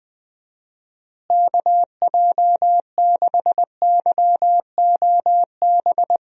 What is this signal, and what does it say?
Morse code, heard as a keyed tone, about 20 wpm, 700 Hz. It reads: KJ6YOB
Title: USA Call Signs - 6 Characters - Repeated 2x - 20wpm